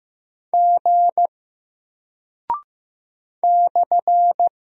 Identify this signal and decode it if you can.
Morse code, heard as a keyed tone, about 15 wpm, 700 Hz. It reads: G /
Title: Single Character (Letter:Number:Punctuation:Prosign) 15wpm